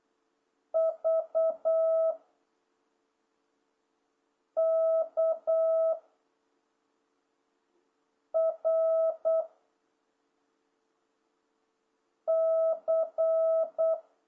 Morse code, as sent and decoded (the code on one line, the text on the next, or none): ...- / -.- / .-. / -.-.
V K R C